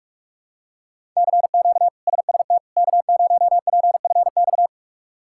Morse code, heard as a keyed tone, about 41 wpm, 700 Hz. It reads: CQ SST K0PFX
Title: K1USN SST QSO Practice - 45wpm